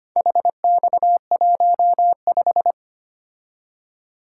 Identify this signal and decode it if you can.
Morse code, heard as a keyed tone, about 25 wpm, 700 Hz. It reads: HX15
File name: practice / 25wpm / Four Letter-Number Combinations - 25wpm